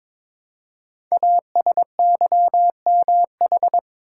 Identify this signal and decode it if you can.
Morse code, heard as a keyed tone, about 23 wpm, 700 Hz. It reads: ASYMH
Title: Code Groups - 5 Letters - 22wpm